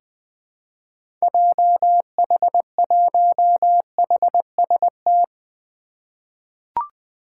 Morse code, heard as a keyed tone, about 20 wpm, 700 Hz. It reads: JH1HST
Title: 2023 POTA Activator Call Signs - 20wpm